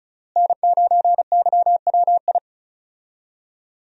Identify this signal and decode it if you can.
Morse code, heard as a keyed tone, about 36 wpm, 700 Hz. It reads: N9YWI
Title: USA Call Signs - 5 Characters - Repeated 3x - 35wpm